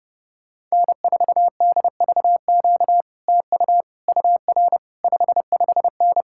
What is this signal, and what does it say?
Morse code, heard as a keyed tone, about 32 wpm, 700 Hz. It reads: N4DVQ TU UR 55N